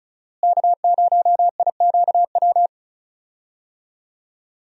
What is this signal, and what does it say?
Morse code, heard as a keyed tone, about 36 wpm, 700 Hz. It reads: K0IQW
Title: USA Call Signs - 5 Characters - Repeated 2x - 35wpm